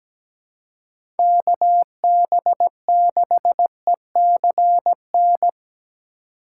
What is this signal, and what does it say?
Morse code, heard as a keyed tone, about 17 wpm, 700 Hz. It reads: KB6ECN
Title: USA Call Signs - 6 Characters - 2x Character Spacing - 17wpm